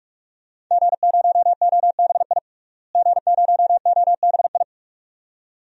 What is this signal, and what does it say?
Morse code, heard as a keyed tone, about 45 wpm, 700 Hz. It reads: G0OBI G0OBI
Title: DXCC-100 Call Signs - 5 Characters - Repeated 2x - 45wpm